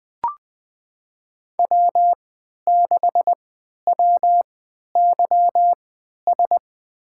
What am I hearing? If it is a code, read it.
Morse code, heard as a keyed tone, about 20 wpm, spaced as for 11 wpm, 700 Hz. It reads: W6WYS
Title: USA Call Signs - 3x Character Spacing - 20wpm